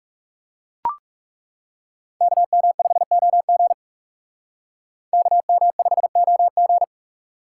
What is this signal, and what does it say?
Morse code, heard as a keyed tone, about 36 wpm, 700 Hz. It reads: KM5OG KM5OG
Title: Speed Racing - USA 2x2 Call Signs 30wpm